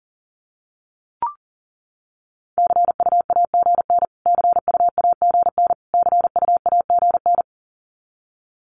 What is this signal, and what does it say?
Morse code, heard as a keyed tone, about 40 wpm, 700 Hz. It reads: CUAGN CUAGN CUAGN
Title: Abbreviations Repeated 3x - 40wpm